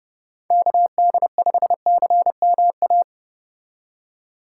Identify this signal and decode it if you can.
Morse code, heard as a keyed tone, about 29 wpm, 700 Hz. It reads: KD5CMA